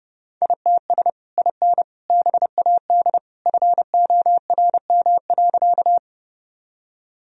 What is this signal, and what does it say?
Morse code, heard as a keyed tone, about 30 wpm, 700 Hz. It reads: ITS IN BAD FORM.
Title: Sentences from Top 1000 Words - Review Part 1 - 30wpm